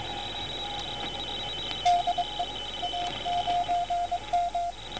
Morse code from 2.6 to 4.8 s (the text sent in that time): A9M